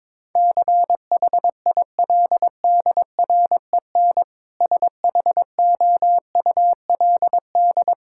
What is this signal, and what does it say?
Morse code, heard as a keyed tone, about 22 wpm, 700 Hz. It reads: CHILDREN SHOULD
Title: Sentences from Top 1000 Words - Review Part 5 - 22wpm